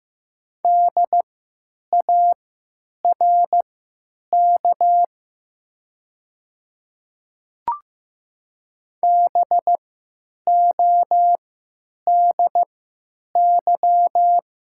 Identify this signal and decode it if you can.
Morse code, heard as a keyed tone, about 15 wpm, spaced as for 9 wpm, 700 Hz. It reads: DARK BODY
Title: 3x Character Spacing - Top 500 Words - 15wpm